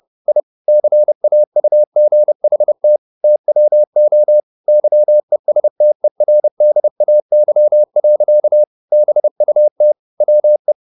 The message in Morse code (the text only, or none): I CAUGHT TWO YESTERDAY. BUT WE